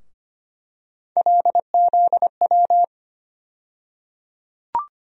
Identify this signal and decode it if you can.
Morse code, heard as a keyed tone, about 25 wpm, 700 Hz. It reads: LZW
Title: USA Call Sign Postfixes - 3 Letters 25wpm